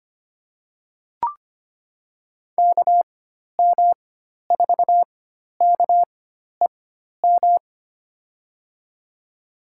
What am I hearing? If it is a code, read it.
Morse code, heard as a keyed tone, about 25 wpm, spaced as for 12 wpm, 700 Hz. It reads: KM4KEM
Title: USA Call Signs - 6 Characters - 4x Character Spacing - 25wpm